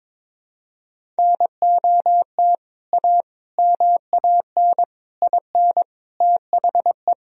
Morse code, heard as a keyed tone, about 22 wpm, 700 Hz. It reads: NOT A MAN IN THE
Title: Sentences from Top 1000 Words - Review Part 1 - 22wpm